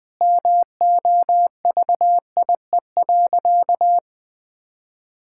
Morse code, heard as a keyed tone, about 20 wpm, 700 Hz. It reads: MOVIE.
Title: Sentences from Top 1000 Words - Review Part 5 - 20wpm